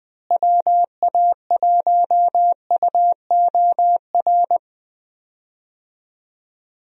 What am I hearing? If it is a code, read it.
Morse code, heard as a keyed tone, about 20 wpm, 700 Hz. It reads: WA1UOR
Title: USA Call Signs - 6 Characters - Repeated 3x - 20wpm